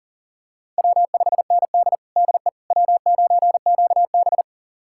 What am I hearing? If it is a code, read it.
Morse code, heard as a keyed tone, about 40 wpm, 700 Hz. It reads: W5ND DE W9QB
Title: USA Callsign DE Callsign - 4 Characters - 40wpm